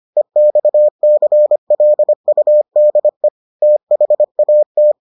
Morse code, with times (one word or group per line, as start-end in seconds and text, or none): 0.2-3.3 EXCLUDE
3.6-4.9 THAT